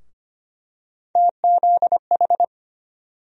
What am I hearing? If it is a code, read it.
Morse code, heard as a keyed tone, about 25 wpm, 700 Hz. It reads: TZH
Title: USA Call Sign Postfixes - 3 Letters 25wpm